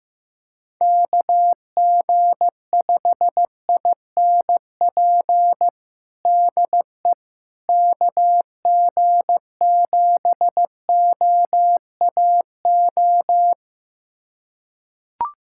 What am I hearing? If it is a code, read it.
Morse code, heard as a keyed tone, about 15 wpm, 700 Hz. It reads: KG5INP DE KG7OAO
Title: USA Callsign DE Callsign - 6 Characters - 15wpm